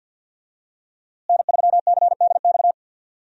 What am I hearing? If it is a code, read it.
Morse code, heard as a keyed tone, about 47 wpm, 700 Hz. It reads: N3CDX
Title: USA Call Signs - 5 Characters - Repeated 2x - 50wpm